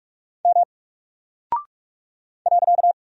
Morse code, heard as a keyed tone, about 44 wpm, 700 Hz. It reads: M .
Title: Single Character (Letter:Number:Punctuation:Prosign) Alt 45wpm